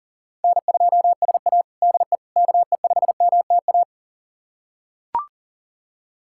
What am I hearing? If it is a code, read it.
Morse code, heard as a keyed tone, about 40 wpm, 700 Hz. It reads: N2SA DE KE5MTA
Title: USA Callsign DE Callsign 40wpm